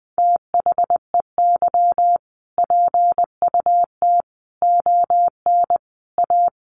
Morse code, heard as a keyed tone, about 20 wpm, 700 Hz. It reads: THEY PUT ON A
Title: Sentences from Top 1200 Words 20wpm